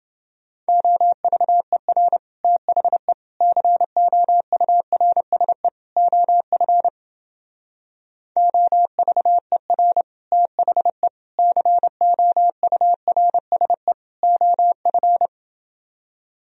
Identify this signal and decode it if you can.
Morse code, heard as a keyed tone, about 29 wpm, 700 Hz. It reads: OVER THE COURSE OF OVER THE COURSE OF